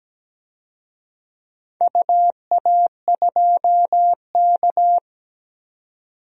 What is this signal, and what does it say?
Morse code, heard as a keyed tone, about 17 wpm, 700 Hz. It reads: UA2K
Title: DXCC-100 Call Signs - 4 Characters - 17wpm